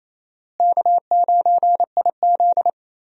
Morse code, heard as a keyed tone, about 28 wpm, 700 Hz. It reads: K9IZ
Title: USA Call Signs - 4 Characters - Repeated 2x - 28wpm